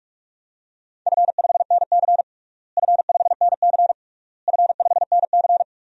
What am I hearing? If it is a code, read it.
Morse code, heard as a keyed tone, about 42 wpm, 700 Hz. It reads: F5NC F5NC F5NC
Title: DXCC-100 Call Signs - 4 Characters - Repeated 3x - 45wpm